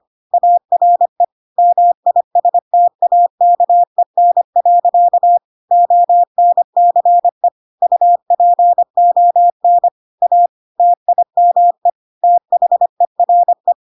Morse code, heard as a keyed tone, about 25 wpm, 700 Hz. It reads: ARE MISTAKEN. ONCE UPON A TIME THERE
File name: Pinocchio / Pinocchio ch1 25 25 ik1hge